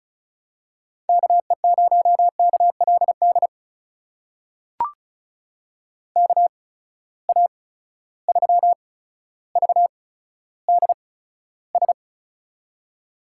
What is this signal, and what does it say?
Morse code, heard as a keyed tone, about 35 wpm, spaced as for 15 wpm, 700 Hz. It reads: KE0KLD KA3VDS
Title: USA Call Signs - 6 Characters - 8x Character Spacing - 35wpm